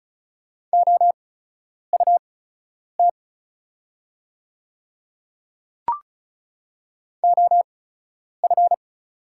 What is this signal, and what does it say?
Morse code, heard as a keyed tone, about 36 wpm, spaced as for 10 wpm, 700 Hz. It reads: OUT OF
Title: 8x Character Spacing - Top 300 Words - 35wpm